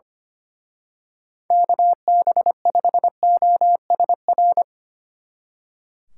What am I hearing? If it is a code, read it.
Morse code, heard as a keyed tone, about 28 wpm, 700 Hz. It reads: KB5OSR
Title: Speed Racing - USA 2x3 Call Signs 25wpm